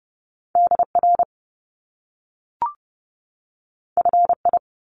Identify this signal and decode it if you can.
Morse code, heard as a keyed tone, about 30 wpm, 700 Hz. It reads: DR FI
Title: Top Two Letter Combinations in Words 30wpm